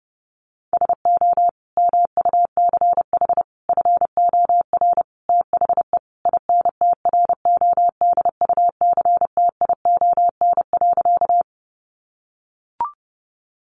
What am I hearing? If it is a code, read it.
Morse code, heard as a keyed tone, about 30 wpm, 700 Hz. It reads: SO MUCH FOR THE INTRODUCTION.
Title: Sentences from Top 1000 Words 30wpm